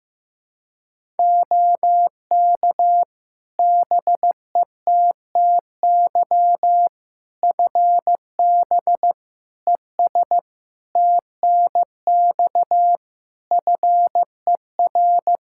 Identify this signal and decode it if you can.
Morse code, heard as a keyed tone, about 15 wpm, 700 Hz. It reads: OK BETTY FB ES TNX FER
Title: Practice QSO - Exchange 2 - Info - 15wpm